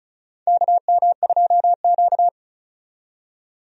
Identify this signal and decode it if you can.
Morse code, heard as a keyed tone, about 34 wpm, 700 Hz. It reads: KM2Q
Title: USA Call Signs - 4 Characters - 35wpm